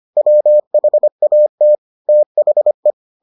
Morse code, heard as a keyed tone, about 25 wpm, 600 Hz. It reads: WHAT THE